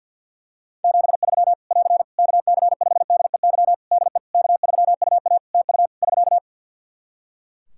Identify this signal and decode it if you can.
Morse code, heard as a keyed tone, about 45 wpm, 700 Hz. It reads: 73 <AR> KC5DEY DE K3UA TU <SK>
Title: Practice QSO - Exchange 4 - Ending - 50wpm